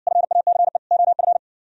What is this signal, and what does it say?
Morse code, heard as a keyed tone, about 57 wpm, 700 Hz. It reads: FACE OF